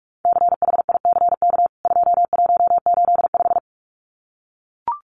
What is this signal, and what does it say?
Morse code, heard as a keyed tone, about 44 wpm, 700 Hz. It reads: CHICK 2185
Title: CWT Exchange 45wpm